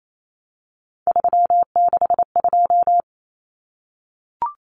Morse code, heard as a keyed tone, about 28 wpm, 700 Hz. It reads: 362
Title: Common Cut Numbers - 3-Digit 28wpm